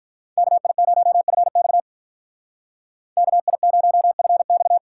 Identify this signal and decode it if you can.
Morse code, heard as a keyed tone, about 42 wpm, 700 Hz. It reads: KI0FX KI0FX
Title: Speed Racing - USA Call Signs 35wpm